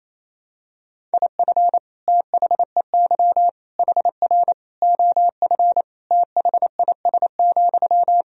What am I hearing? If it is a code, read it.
Morse code, heard as a keyed tone, about 27 wpm, 700 Hz. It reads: IF THEY HR OF THIS,
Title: Sentences from Top 400 Words and Abbreviations 28wpm